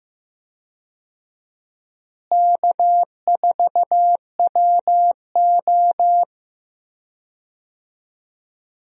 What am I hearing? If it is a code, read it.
Morse code, heard as a keyed tone, about 15 wpm, 700 Hz. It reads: K4WO
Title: USA Call Signs - 4 Characters - Repeated 2x - 15wpm